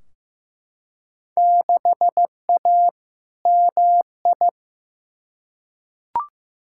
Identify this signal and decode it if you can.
Morse code, heard as a keyed tone, about 15 wpm, 700 Hz. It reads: 6A MI
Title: Field Day Exchanges - 15wpm